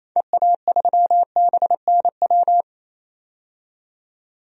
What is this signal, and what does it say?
Morse code, heard as a keyed tone, about 28 wpm, 700 Hz. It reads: EA3BNW
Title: DXCC-100 Call Signs - 6 Characters - 28wpm